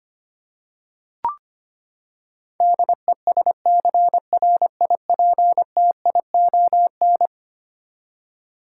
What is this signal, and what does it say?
Morse code, heard as a keyed tone, about 25 wpm, 700 Hz. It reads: DESCRIPTION
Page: morse-code-ninja-practice-25wpm